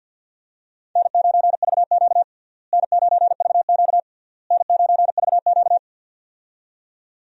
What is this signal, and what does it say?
Morse code, heard as a keyed tone, about 51 wpm, 700 Hz. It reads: N9VQ N9VQ N9VQ
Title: USA Call Signs - 4 Characters - Repeated 3x - 50wpm